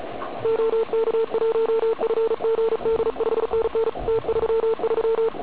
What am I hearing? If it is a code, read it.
Morse code, heard as a keyed tone, about 33 wpm, 440 Hz, with radio noise. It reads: OK1FGD 5NN T33
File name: Archiv / CUC106 / cuc106 PILEUP